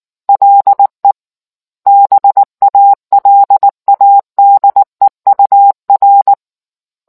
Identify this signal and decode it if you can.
Morse code, heard as a keyed tone, about 19 wpm, 800 Hz. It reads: LE BALADEUR